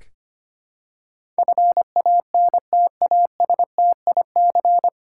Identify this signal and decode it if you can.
Morse code, heard as a keyed tone, about 25 wpm, 700 Hz. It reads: FANTASTIC
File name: practice / 25wpm / 9 Letter Words - First 3 Letters Given - 25wpm